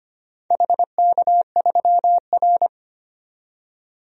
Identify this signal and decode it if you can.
Morse code, heard as a keyed tone, about 25 wpm, 700 Hz. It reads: HK3R
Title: DXCC-100 Call Signs - 4 Characters - 25wpm